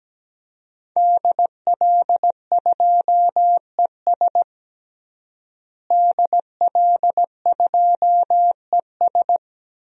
Morse code, heard as a keyed tone, about 17 wpm, 700 Hz. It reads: DL2ES DL2ES
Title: DXCC-100 Call Signs - 5 Characters - Repeated 2x - 17wpm